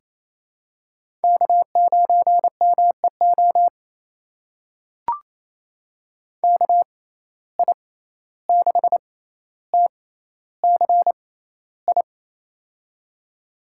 Morse code, heard as a keyed tone, about 28 wpm, spaced as for 14 wpm, 700 Hz. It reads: K9MEO KI6TCI
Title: USA Call Signs - 6x Character Spacing - 28wpm